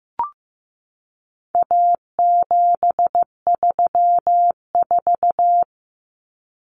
Morse code, heard as a keyed tone, about 15 wpm, 700 Hz. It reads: A734
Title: Common Cut Numbers - 4-Digit 15wpm